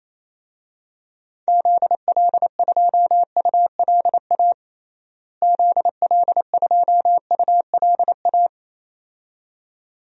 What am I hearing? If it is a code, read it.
Morse code, heard as a keyed tone, about 28 wpm, 700 Hz. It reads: ZL2ULA ZL2ULA
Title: DXCC-100 Call Signs - 6 Characters - Repeated 2x - 28wpm